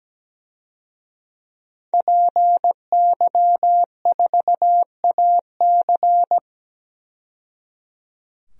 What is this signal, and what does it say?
Morse code, heard as a keyed tone, about 17 wpm, 700 Hz. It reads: PY4AC
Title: DXCC-100 Call Signs - 5 Characters - 17wpm